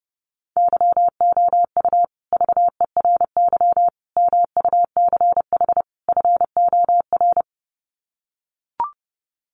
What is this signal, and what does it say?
Morse code, heard as a keyed tone, about 30 wpm, 700 Hz. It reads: YOU VERY MUCH FOR
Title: Sets of 4 Words 30wpm